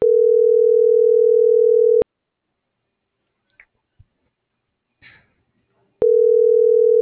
Ringback tone (North America), 440 + 480 Hz.
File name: CENTRAL MABE / out antonio.garcia